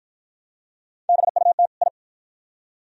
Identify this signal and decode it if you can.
Morse code, heard as a keyed tone, about 52 wpm, 700 Hz. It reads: BUT I